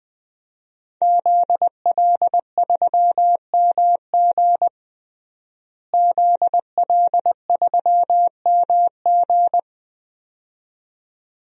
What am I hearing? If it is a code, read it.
Morse code, heard as a keyed tone, about 21 wpm, 700 Hz. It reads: ZL3MG ZL3MG
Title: DXCC-100 Call Signs - 5 Characters - Repeated 2x - 20wpm